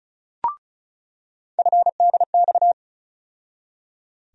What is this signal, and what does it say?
Morse code, heard as a keyed tone, about 36 wpm, 700 Hz. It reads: FDX
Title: USA Call Sign Postfixes - 3 Letters 35wpm